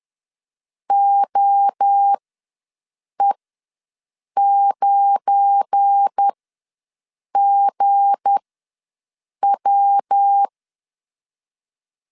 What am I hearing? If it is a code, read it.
Morse code, heard as a keyed tone, about 10 wpm, spaced as for 6 wpm, 800 Hz. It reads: OE9GW
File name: LEZIONI / 7 GF9Z / 7 GF9 ws 10wpm 4034